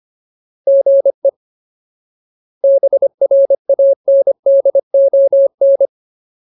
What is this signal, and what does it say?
Morse code, heard as a keyed tone, about 25 wpm, 550 Hz. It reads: GE BRANDON
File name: Ultimate Word List / 25WPM Ultimate